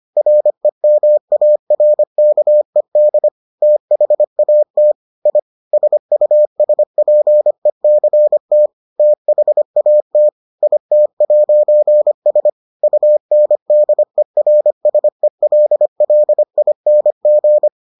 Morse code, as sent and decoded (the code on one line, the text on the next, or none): .-. -- .- .-. -.- . -.. / - .... .- - / .. / ... ..- ... .--. -.-. - / - .... .- - / .. - .----. ... / ..- -. -.. .-. ... .-.. .-.. .. -. --.
REMARKED THAT I SUSPECT THAT IT'S UNDERSELLING